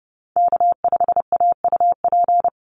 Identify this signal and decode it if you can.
Morse code, heard as a keyed tone, about 30 wpm, 700 Hz. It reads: K5AUP
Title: Call Signs Repeated 3x - 30wpm